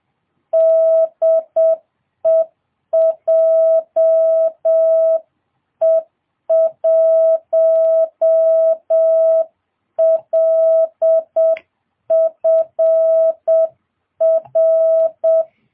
Morse code, heard as a keyed tone, about 7 wpm, 660 Hz. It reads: DE JE1LFR